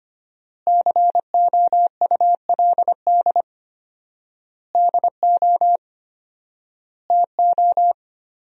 Morse code, heard as a keyed tone, about 25 wpm, 700 Hz. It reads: COULD DO TO
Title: 4x Word-Spacing - Sentences from Top 400 Words - 25wpm